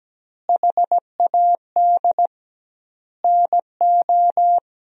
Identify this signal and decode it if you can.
Morse code, heard as a keyed tone, about 17 wpm, 700 Hz. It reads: HAD NO
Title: 2x Word-Spacing - Sentences from Top 100 Words - 17wpm